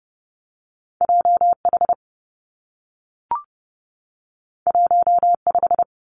Morse code, heard as a keyed tone, about 32 wpm, 700 Hz. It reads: J H 1 5
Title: Two Challenging Letter-Number Combinations 30wpm